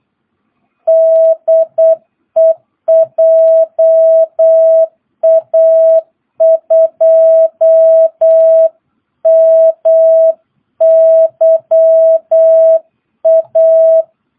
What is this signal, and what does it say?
Morse code, heard as a keyed tone, about 8 wpm, 660 Hz. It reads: DE JA2MYA
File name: jh8cbh / 2022cw